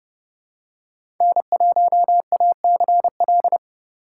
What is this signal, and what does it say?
Morse code, heard as a keyed tone, about 32 wpm, 700 Hz. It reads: N1ACL
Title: USA Call Signs - 5 Characters - 6x Character Spacing - 30wpm